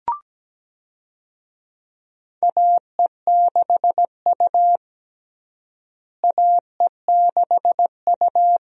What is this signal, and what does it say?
Morse code, heard as a keyed tone, about 17 wpm, 700 Hz. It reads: AE6U AE6U
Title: USA Call Signs - 4 Characters - Repeated 2x - 17wpm